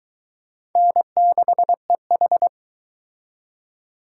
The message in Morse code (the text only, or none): N6EH